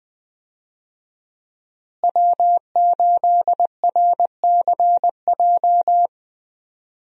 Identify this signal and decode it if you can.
Morse code, heard as a keyed tone, about 20 wpm, 700 Hz. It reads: W8RCJ
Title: USA Call Signs - 5 Characters - Repeated 3x - 20wpm